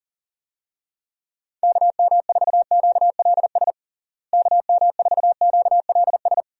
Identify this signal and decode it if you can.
Morse code, heard as a keyed tone, about 37 wpm, 700 Hz. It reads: KM4QLS KM4QLS
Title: USA Call Signs - 6 Characters - Repeated 2x - 40wpm